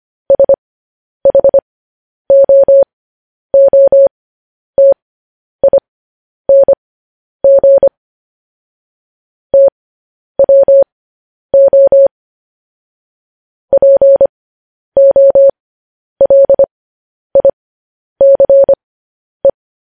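Morse code, heard as a keyed tone, about 25 wpm, spaced as for 10 wpm, 550 Hz. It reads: SHOOTING TWO POLICE
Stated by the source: Morse code news headlines, likely sent at 10 WPM.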